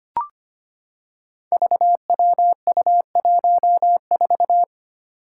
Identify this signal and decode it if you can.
Morse code, heard as a keyed tone, about 25 wpm, 700 Hz. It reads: VWU14
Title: Five Challenging Letter-Number Combinations 25wpm